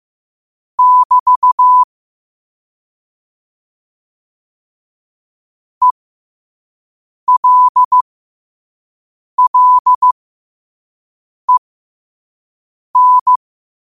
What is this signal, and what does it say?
Morse code, heard as a keyed tone, about 15 wpm, spaced as for 5 wpm, 1000 Hz. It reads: <BT> ELLEN
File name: qotdfiles / 5wpm